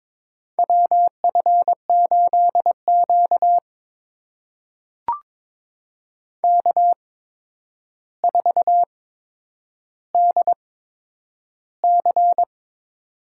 Morse code, heard as a keyed tone, about 22 wpm, spaced as for 9 wpm, 700 Hz. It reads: WF8Q K4DC
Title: USA Call Signs - 4 Characters - 8x Character Spacing - 22wpm